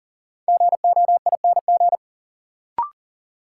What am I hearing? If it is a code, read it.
Morse code, heard as a keyed tone, about 40 wpm, 700 Hz. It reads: GOING
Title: Sets of 2 Words - New Word Drill - 40wpm